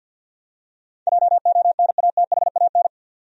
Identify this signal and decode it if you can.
Morse code, heard as a keyed tone, about 49 wpm, 700 Hz. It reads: JONATHAN
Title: USA Common Names - 50wpm